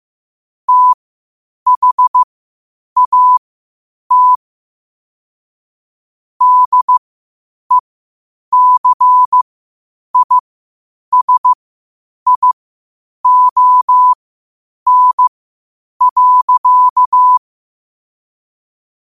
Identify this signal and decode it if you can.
Morse code, heard as a keyed tone, about 15 wpm, spaced as for 8 wpm, 1000 Hz. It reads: THAT DECISION.
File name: qotdfiles / 8wpm